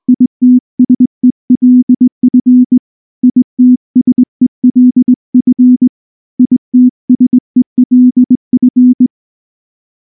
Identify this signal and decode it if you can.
Morse code, heard as a keyed tone, about 19 wpm, 250 Hz. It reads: ITSELF ITSELF ITSELF
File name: morse / morse-training-20WPM-5